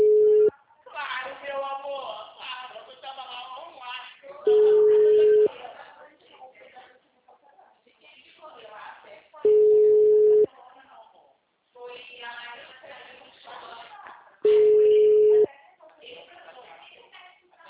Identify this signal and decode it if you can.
Ringback tone (Europe, 425 Hz), with speech.
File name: out- -2014- - - .10691